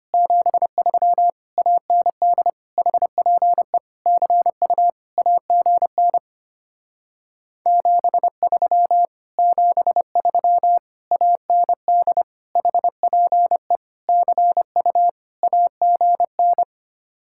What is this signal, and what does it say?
Morse code, heard as a keyed tone, about 25 wpm, 700 Hz. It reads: 73 AND HPE CU AGN 73 73 AND HPE CU AGN
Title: Speed-Racing Common QSO Phrases - 25wpm